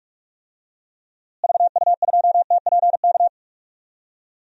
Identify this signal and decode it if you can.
Morse code, heard as a keyed tone, about 46 wpm, 700 Hz. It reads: VU2TPK
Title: DXCC-100 Call Signs - 6 Characters - Repeated 3x - 45wpm